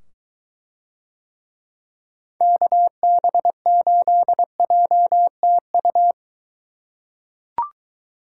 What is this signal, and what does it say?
Morse code, heard as a keyed tone, about 21 wpm, 700 Hz. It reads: KB8JTU